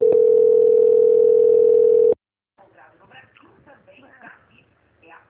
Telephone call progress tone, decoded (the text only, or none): ringback tone